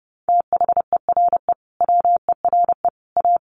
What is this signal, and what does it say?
Morse code, heard as a keyed tone, about 30 wpm, 700 Hz. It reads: THERE WERE A
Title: Sets of 5 Words Encore2 30wpm